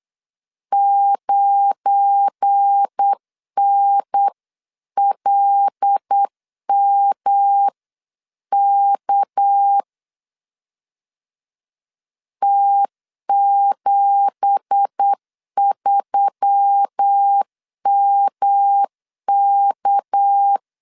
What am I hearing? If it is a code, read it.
Morse code, heard as a keyed tone, about 8 wpm, 800 Hz. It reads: 9NLMK T73MK
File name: CW / DET / 1 9 1500C S B 8wpm 5624